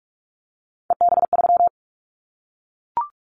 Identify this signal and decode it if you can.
Morse code, heard as a keyed tone, about 44 wpm, 700 Hz. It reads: EB3
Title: Callsign Prefixes for Top 25 DXCC - Repeated 3x - 45wpm